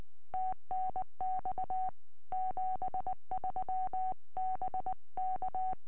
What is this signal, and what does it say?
Morse code, heard as a keyed tone, about 20 wpm, 750 Hz. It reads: TNX 73 BK
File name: tropo / f6ape 2m